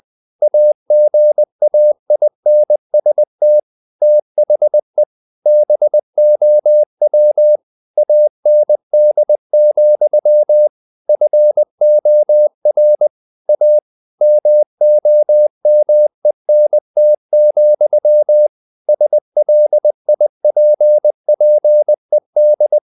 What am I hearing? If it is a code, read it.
Morse code, heard as a keyed tone, about 20 wpm, 600 Hz. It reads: AGAINST THE BOW AND, FOR A MOMENT, SLIPPED
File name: the old man and the sea part 2 20wpm